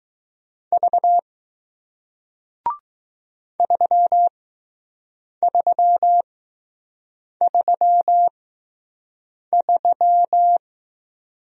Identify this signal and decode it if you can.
Morse code, heard as a keyed tone, about 20 wpm, spaced as for 6 wpm, 700 Hz. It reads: V 3333